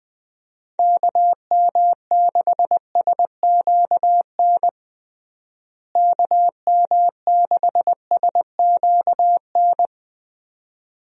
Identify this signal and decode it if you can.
Morse code, heard as a keyed tone, about 21 wpm, 700 Hz. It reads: KM6SQN KM6SQN
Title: USA Call Signs - 6 Characters - Repeated 2x - 20wpm